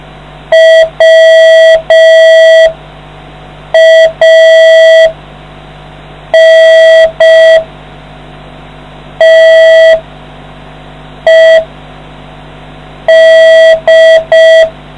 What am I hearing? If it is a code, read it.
Morse code, heard as a keyed tone, about 5 wpm, spaced as for 4 wpm, 650 Hz. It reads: WANTED